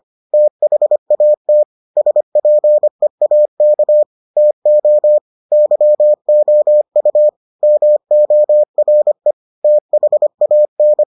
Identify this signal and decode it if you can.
Morse code, heard as a keyed tone, about 25 wpm, 600 Hz. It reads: THAT SPEAK TO YOU MORE THAN